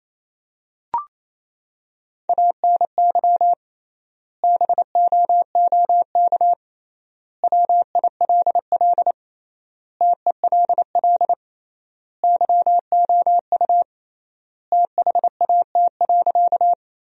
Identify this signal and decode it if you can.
Morse code, heard as a keyed tone, about 28 wpm, 700 Hz. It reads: ANY BOOK WILL TELL YOU THAT.
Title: 3x Word-Spacing - Sentences from Top 300 Words - 28wpm